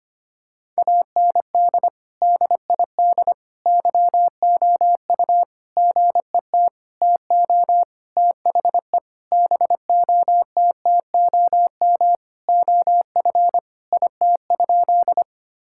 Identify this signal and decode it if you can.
Morse code, heard as a keyed tone, about 25 wpm, 700 Hz. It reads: AND DID YOU GET TO THE BOTTOM OF IT?
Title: Sentences from Top 1000 Words - Review Part 6 - 25wpm